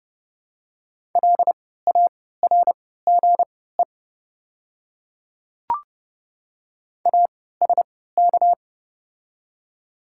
Morse code, heard as a keyed tone, about 30 wpm, spaced as for 17 wpm, 700 Hz. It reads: LARGE ASK
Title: 3x Character Spacing - Top 200 Words - 30wpm